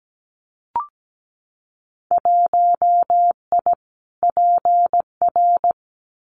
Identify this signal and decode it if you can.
Morse code, heard as a keyed tone, about 17 wpm, 700 Hz. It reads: 1I PR
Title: Winter Field Day Exchanges Encore 17wpm